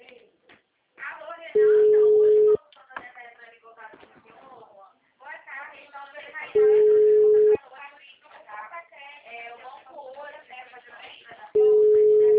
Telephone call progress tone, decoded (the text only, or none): ringback tone